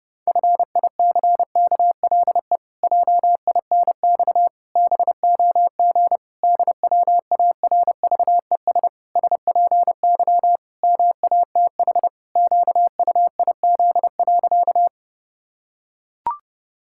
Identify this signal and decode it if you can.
Morse code, heard as a keyed tone, about 30 wpm, 700 Hz. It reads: FICKLE JINX BOG DWARVES SPY MATH QUIZ.
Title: Pangram Challenge v2 - 30wpm